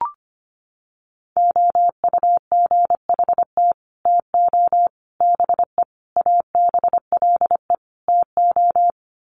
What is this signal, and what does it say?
Morse code, heard as a keyed tone, about 25 wpm, 700 Hz. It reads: OUGHT TO BE ABLE TO
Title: Sets of 5 Words Encore2 25wpm